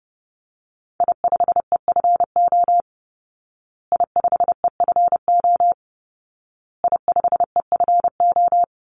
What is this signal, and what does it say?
Morse code, heard as a keyed tone, about 30 wpm, 700 Hz. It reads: I5EFO I5EFO I5EFO
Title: Call Signs Repeated 2x - 30wpm